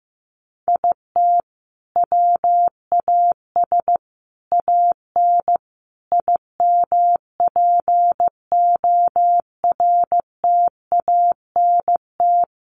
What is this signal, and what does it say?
Morse code, heard as a keyed tone, about 15 wpm, 700 Hz. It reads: IT WAS AN IMPORTANT
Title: Sentences from Top 1400 Words 15wpm